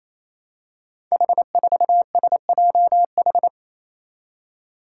Morse code, heard as a keyed tone, about 27 wpm, 700 Hz. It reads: H4SJH
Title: Five Challenging Letter-Number Combinations 28wpm